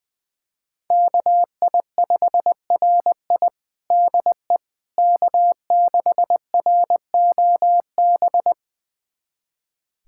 Morse code, heard as a keyed tone, about 20 wpm, 700 Hz. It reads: KI5RI DE K6ROB
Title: USA Callsign DE Callsign - 5 Characters - 20wpm